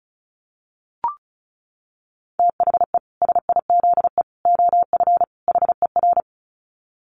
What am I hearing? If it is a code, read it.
Morse code, heard as a keyed tone, about 34 wpm, 700 Hz. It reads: THE SIZE OF HER
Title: Dit Challenge 35wpm